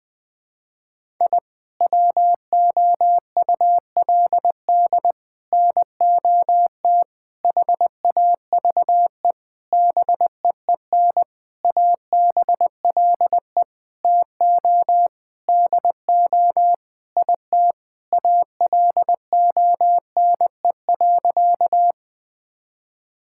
Morse code, heard as a keyed tone, about 20 wpm, 700 Hz. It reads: I WOULD NOT HAVE BEEN ABLE TO DO IT ALONE.
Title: Sentences with 6-Word Phrases - 20wpm